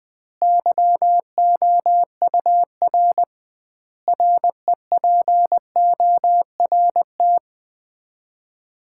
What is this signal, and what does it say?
Morse code, heard as a keyed tone, about 20 wpm, 700 Hz. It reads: YOUR REPORT
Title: Common QSO Phrases - 2x Word-Spacing - 20wpm